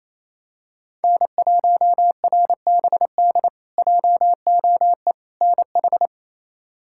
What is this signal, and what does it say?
Morse code, heard as a keyed tone, about 28 wpm, 700 Hz. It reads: N1RBD JOE NH
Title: K1USN SST QSO Practice - 28wpm